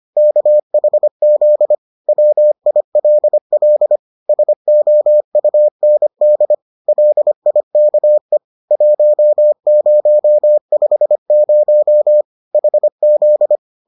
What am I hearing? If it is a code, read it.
Morse code, heard as a keyed tone, about 25 wpm, 600 Hz. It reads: KHZ WILL SOUND LIKE 1050 HZ